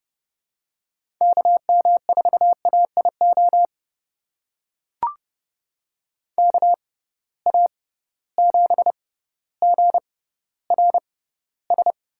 Morse code, heard as a keyed tone, about 31 wpm, spaced as for 16 wpm, 700 Hz. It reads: KM4AIO KA7GRS
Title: USA Call Signs - 6 Characters - 6x Character Spacing - 30wpm